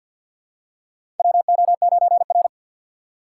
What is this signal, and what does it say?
Morse code, heard as a keyed tone, about 46 wpm, 700 Hz. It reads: WO9R
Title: USA Call Signs - 3x Character Spacing - 50wpm